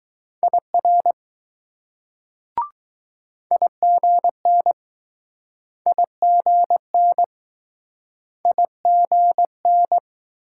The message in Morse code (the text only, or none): IR IGN IGN IGN